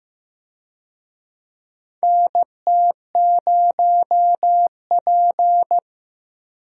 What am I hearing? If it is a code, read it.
Morse code, heard as a keyed tone, about 15 wpm, 700 Hz. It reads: NT0P
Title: USA Call Signs - 4 Characters - Repeated 3x - 15wpm